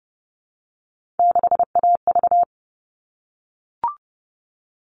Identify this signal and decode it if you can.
Morse code, heard as a keyed tone, about 30 wpm, 700 Hz. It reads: 6AV
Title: All Cut Numbers - 3-Digit 30wpm